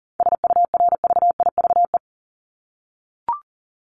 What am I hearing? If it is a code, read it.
Morse code, heard as a keyed tone, about 40 wpm, 700 Hz. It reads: SURVIVE
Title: Top 1251-1500 Words 40wpm